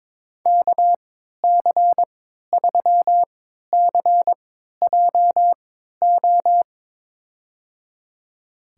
Morse code, heard as a keyed tone, about 22 wpm, spaced as for 12 wpm, 700 Hz. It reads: KC3CJO
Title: USA Call Signs - 6 Characters - 3x Character Spacing - 22wpm